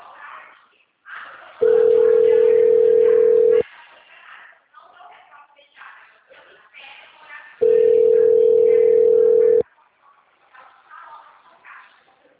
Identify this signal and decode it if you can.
Ringback tone (North America), 440 + 480 Hz.